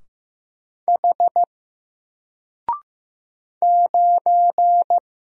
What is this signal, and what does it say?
Morse code, heard as a keyed tone, about 15 wpm, 700 Hz. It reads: H 9
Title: Single Letter-Number Alt 15wpm